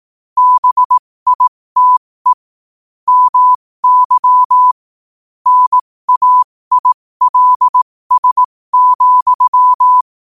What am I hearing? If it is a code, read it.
Morse code, heard as a keyed tone, about 18 wpm, spaced as for 15 wpm, 1000 Hz. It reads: BITE MY NAILS,
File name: qotdfiles / 15wpm